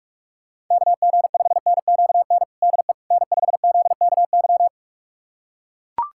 Morse code, heard as a keyed tone, about 42 wpm, 700 Hz. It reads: KG5NQN DE N5ZKY
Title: USA Callsign DE Callsign 45wpm